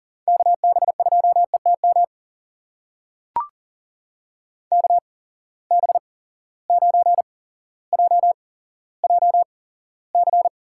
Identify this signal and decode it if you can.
Morse code, heard as a keyed tone, about 40 wpm, spaced as for 17 wpm, 700 Hz. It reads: KB2ETM KB9JJC
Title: USA Call Signs - 6 Characters - 8x Character Spacing - 40wpm